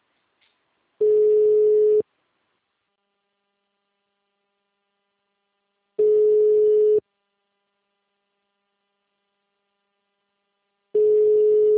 Ringback tone (Europe, 425 Hz).
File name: out- -2019- - - .7227